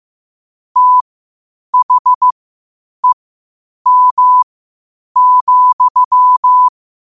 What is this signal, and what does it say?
Morse code, heard as a keyed tone, about 15 wpm, spaced as for 8 wpm, 1000 Hz. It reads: THEM,